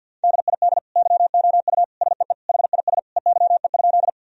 Morse code, heard as a keyed tone, about 47 wpm, 700 Hz. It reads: DID YOU SEE HIS EYE?
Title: Sentences from Top 1000 Words - Review Part 1 - 50wpm